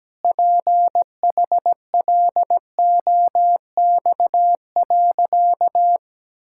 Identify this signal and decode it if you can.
Morse code, heard as a keyed tone, about 17 wpm, 700 Hz. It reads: PHLOX.
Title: Pangram Challenge v2 - 17wpm